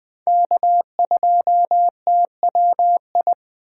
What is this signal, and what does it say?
Morse code, heard as a keyed tone, about 20 wpm, 700 Hz. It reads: K2TWI
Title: Speed Racing - USA 1x3 Call Signs 17wpm